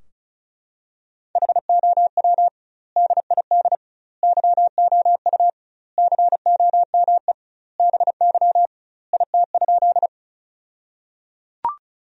Morse code, heard as a keyed tone, about 36 wpm, 700 Hz. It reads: HOW DID YOU COME BY IT?
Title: 2x Word-Spacing - Sentences from Top 100 Words - 35wpm